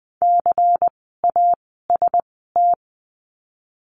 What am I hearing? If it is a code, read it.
Morse code, heard as a keyed tone, about 20 wpm, spaced as for 15 wpm, 700 Hz. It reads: CAST
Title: 094 Lesson 16f - Letter c - Words 20wpm